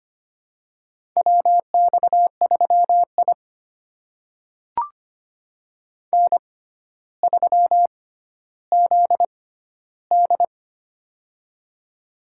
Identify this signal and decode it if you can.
Morse code, heard as a keyed tone, about 25 wpm, spaced as for 13 wpm, 700 Hz. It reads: WX3I N3ZD
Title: USA Call Signs - 4 Characters - 6x Character Spacing - 25wpm